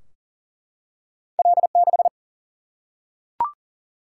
Morse code, heard as a keyed tone, about 38 wpm, 700 Hz. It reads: L6
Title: Single-Letter Single-Number 40wpm